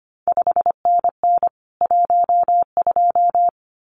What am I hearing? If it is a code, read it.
Morse code, heard as a keyed tone, about 25 wpm, 700 Hz. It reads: 5NN 12
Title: CQ WW DX Contest Exchange Encore - 25wpm